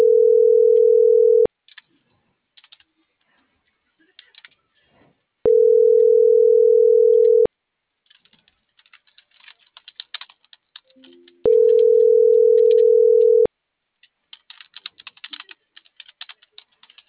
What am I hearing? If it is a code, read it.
Ringback tone (North America), 440 + 480 Hz.
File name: CENTRAL NMP / out melanie.diaznmp - .38540